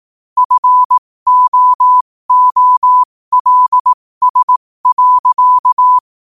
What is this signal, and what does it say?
Morse code, heard as a keyed tone, about 18 wpm, spaced as for 15 wpm, 1000 Hz. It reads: FOOLS.
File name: qotdfiles / 15wpm